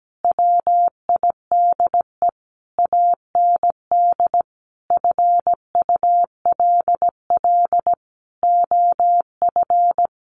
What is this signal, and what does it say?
Morse code, heard as a keyed tone, about 17 wpm, 700 Hz. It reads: WIDE AND FULL OF
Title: A Sound Of Thunder 17wpm